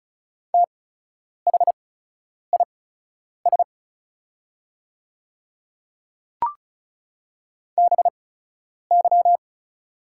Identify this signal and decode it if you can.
Morse code, heard as a keyed tone, about 36 wpm, spaced as for 10 wpm, 700 Hz. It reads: THIS BY
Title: 8x Character Spacing - Top 100 Words - 35wpm